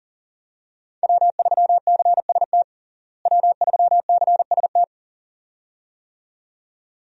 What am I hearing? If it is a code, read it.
Morse code, heard as a keyed tone, about 37 wpm, 700 Hz. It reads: W3CST W3CST
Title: USA Call Signs - 5 Characters - Repeated 2x - 40wpm